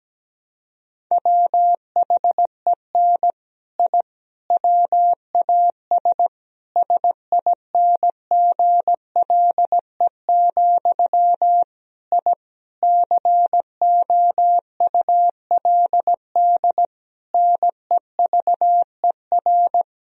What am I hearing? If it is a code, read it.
Morse code, heard as a keyed tone, about 17 wpm, 700 Hz. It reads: WHEN I WAS SINGLE, I COULD NEVER
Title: Sentences from Top 1000 Words - Review Part 4 - 17wpm